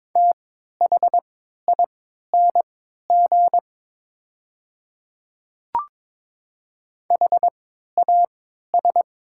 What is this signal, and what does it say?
Morse code, heard as a keyed tone, about 22 wpm, spaced as for 13 wpm, 700 Hz. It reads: THING HAS
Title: 3x Character Spacing - Top 100 Words - 22wpm